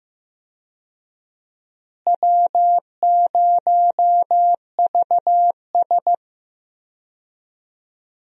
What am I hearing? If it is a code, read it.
Morse code, heard as a keyed tone, about 15 wpm, 700 Hz. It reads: W0VS
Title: USA Call Signs - 4 Characters - Repeated 2x - 15wpm